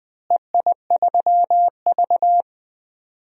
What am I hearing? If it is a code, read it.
Morse code, heard as a keyed tone, about 20 wpm, 700 Hz. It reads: EI3V
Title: DXCC-100 Call Signs - 4 Characters - Repeated 3x - 20wpm